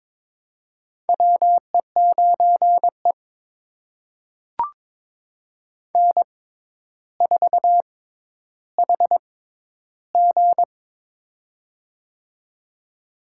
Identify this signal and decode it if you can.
Morse code, heard as a keyed tone, about 22 wpm, spaced as for 11 wpm, 700 Hz. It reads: WE9E N4HG
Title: USA Call Signs - 4 Characters - 6x Character Spacing - 22wpm